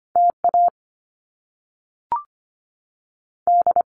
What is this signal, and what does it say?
Morse code, heard as a keyed tone, about 25 wpm, spaced as for 3 wpm, 700 Hz. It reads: T AD
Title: Top Two Letter Combinations in Words 25wpm